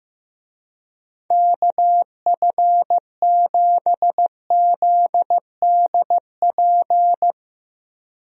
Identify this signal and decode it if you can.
Morse code, heard as a keyed tone, about 15 wpm, 700 Hz. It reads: KF7ZDP
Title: USA Call Signs - 6 Characters - Repeated 3x - 15wpm